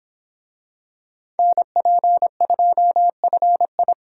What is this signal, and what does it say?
Morse code, heard as a keyed tone, about 25 wpm, 700 Hz. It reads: NP2FI